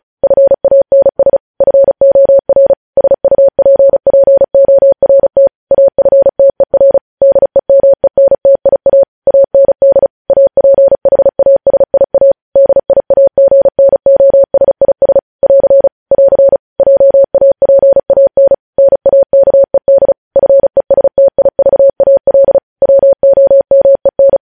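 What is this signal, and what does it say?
Morse code, heard as a keyed tone, about 36 wpm, 550 Hz. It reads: FANS FOR SUPPORT AFTER DEMENTIA AND APHASIA DIAGNOSIS <AR> <AR> JAPAN NAKED FESTIVAL WOMEN
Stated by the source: Morse code transmission of news headlines, preceded by vvv.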